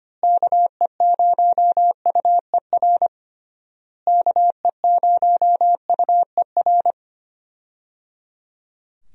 Morse code, heard as a keyed tone, about 25 wpm, 700 Hz. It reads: KE0UER KE0UER
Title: USA Call Signs - 6 Characters - Repeated 3x - 25wpm